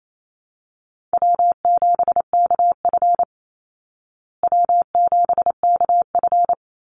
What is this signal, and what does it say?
Morse code, heard as a keyed tone, about 28 wpm, 700 Hz. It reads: W7KF W7KF
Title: Easy Call Signs Repeated 2x - 28wpm